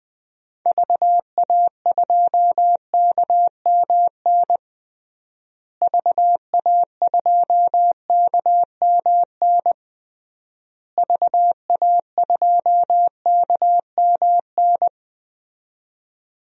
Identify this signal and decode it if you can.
Morse code, heard as a keyed tone, about 21 wpm, 700 Hz. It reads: VA2KMN VA2KMN VA2KMN
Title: DXCC-100 Call Signs - 6 Characters - Repeated 3x - 20wpm